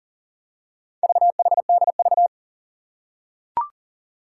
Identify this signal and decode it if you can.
Morse code, heard as a keyed tone, about 38 wpm, 700 Hz. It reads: VHDV
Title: Four Challenging Letter-Number Combinations 40wpm